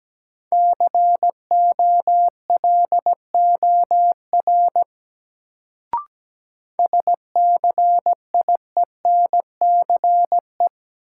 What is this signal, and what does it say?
Morse code, heard as a keyed tone, about 17 wpm, 700 Hz. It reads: COLOR SCIENCE
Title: Sentences from Top 1000 Words - Review Part 2 - New Word Drill - 17wpm